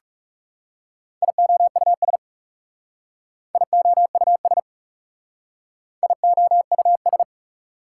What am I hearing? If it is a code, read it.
Morse code, heard as a keyed tone, about 37 wpm, 700 Hz. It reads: IOUS IOUS IOUS
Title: Speed-Racing Prefixes and Suffixes 30wpm